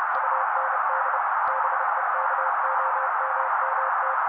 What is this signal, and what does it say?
Morse code, heard as a keyed tone, about 30 wpm, 530 Hz. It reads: UND BEKOMMT
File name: daily / 29wpm noise